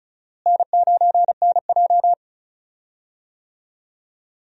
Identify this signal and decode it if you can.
Morse code, heard as a keyed tone, about 34 wpm, 700 Hz. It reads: N9NJ
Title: USA Call Signs - 4 Characters - Repeated 2x - 35wpm